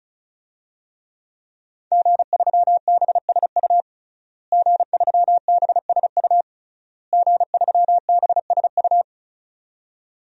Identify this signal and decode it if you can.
Morse code, heard as a keyed tone, about 35 wpm, 700 Hz. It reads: G3BSU G3BSU G3BSU
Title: DXCC-100 Call Signs - 5 Characters - Repeated 3x - 35wpm